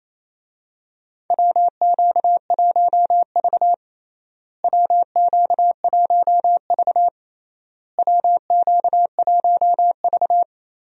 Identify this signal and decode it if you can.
Morse code, heard as a keyed tone, about 28 wpm, 700 Hz. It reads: WQ1V WQ1V WQ1V
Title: USA Call Signs - 4 Characters - Repeated 3x - 28wpm